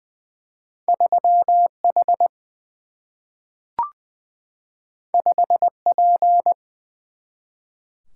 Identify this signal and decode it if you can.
Morse code, heard as a keyed tone, about 20 wpm, 700 Hz. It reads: 3 H 5 P
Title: Single-Letter Single-Number 20wpm